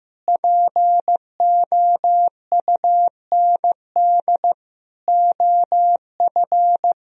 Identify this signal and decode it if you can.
Morse code, heard as a keyed tone, about 15 wpm, 700 Hz. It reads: POUND OF
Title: Sentences from Top 1000 Words - Review Part 3 - 15wpm